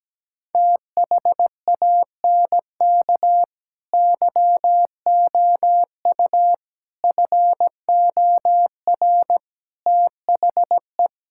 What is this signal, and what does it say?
Morse code, heard as a keyed tone, about 17 wpm, 700 Hz. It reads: THANK YOU FOR THE
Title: Sentences from Top 1600 Words 17wpm